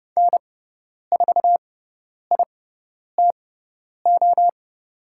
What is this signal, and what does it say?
Morse code, heard as a keyed tone, about 30 wpm, spaced as for 10 wpm, 700 Hz. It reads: N4ITO
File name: Lesson 18g - Letter m - Call Signs 30wpm